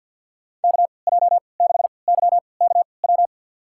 Morse code, heard as a keyed tone, about 52 wpm, spaced as for 28 wpm, 690 Hz. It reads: KJ6YKW
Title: USA Call Signs - 3x Character Spacing - 50wpm